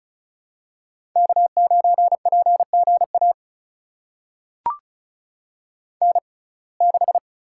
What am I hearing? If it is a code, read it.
Morse code, heard as a keyed tone, about 34 wpm, spaced as for 25 wpm, 700 Hz. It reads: K9PGA N6
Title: USA Call Signs - 5 Characters - 6x Character Spacing - 35wpm